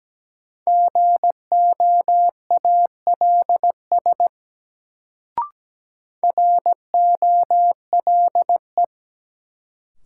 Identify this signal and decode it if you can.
Morse code, heard as a keyed tone, about 17 wpm, 700 Hz. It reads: GOALS ROLE
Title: Sentences from Top 1000 Words - Review Part 6 - New Word Drill - 17wpm